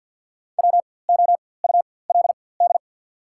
Morse code, heard as a keyed tone, about 54 wpm, spaced as for 24 wpm, 700 Hz. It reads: WOULD
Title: 4x Character Spacing - Top 200 Words - 50wpm